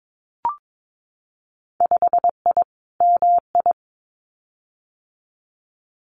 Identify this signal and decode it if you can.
Morse code, heard as a keyed tone, about 22 wpm, 700 Hz. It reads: 5I MI
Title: Winter Field Day Exchanges Encore 22wpm